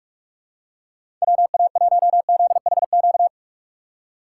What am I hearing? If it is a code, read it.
Morse code, heard as a keyed tone, about 46 wpm, 700 Hz. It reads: WA1ZHQ
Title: USA Call Signs - 6 Characters - 2x Character Spacing - 45wpm